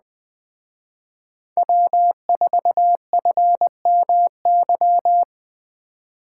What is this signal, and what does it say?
Morse code, heard as a keyed tone, about 20 wpm, 700 Hz. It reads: W4FMY
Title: Speed Racing - USA 1x3 Call Signs 17wpm